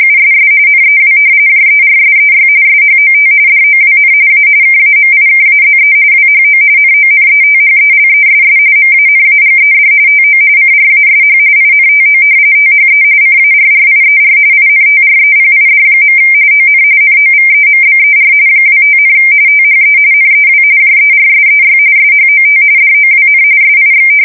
RTTY, about 45.45 baud. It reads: USS NEW JERSEY SHOT DOWN A KAMIKAZE WHO NONETHELESS MANAGED TO CRASH INTO THE CARRIER INTREPID. ON NOVEMBER 25, THE NEW JERSEY SHOT DOWN SIX
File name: fsk170 / battleshipnj